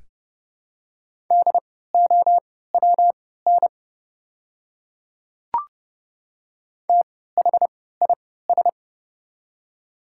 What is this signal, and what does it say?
Morse code, heard as a keyed tone, about 31 wpm, spaced as for 17 wpm, 700 Hz. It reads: DOWN THIS
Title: 3x Character Spacing - Top 200 Words - 30wpm